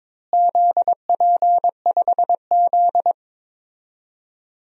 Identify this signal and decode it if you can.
Morse code, heard as a keyed tone, about 22 wpm, 700 Hz. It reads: ZP5Z
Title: DXCC-100 Call Signs - 4 Characters - Repeated 2x - 22wpm